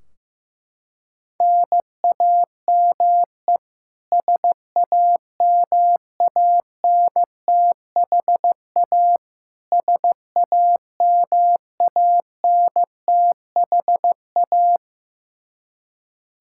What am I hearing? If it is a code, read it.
Morse code, heard as a keyed tone, about 15 wpm, 700 Hz. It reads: NAME SAMANTHA SAMANTHA
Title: Practice QSO - Exchange 1 - Beginning - 15wpm